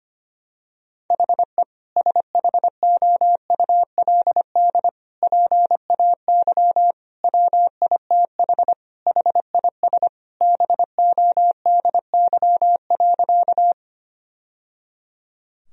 Morse code, heard as a keyed tone, about 25 wpm, 700 Hz. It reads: HE SHOULD PAY WITH HIS BODY.
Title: Sentences from Top 1000 Words - Review Part 2 - 25wpm